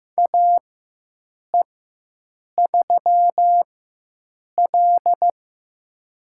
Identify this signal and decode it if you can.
Morse code, heard as a keyed tone, about 15 wpm, spaced as for 7 wpm, 700 Hz. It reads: AE3L
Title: USA Call Signs - 4x Character Spacing - 15wpm